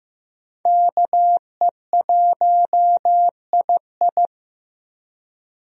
Morse code, heard as a keyed tone, about 15 wpm, 700 Hz. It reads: KE1II
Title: USA Call Signs - 5 Characters - 15wpm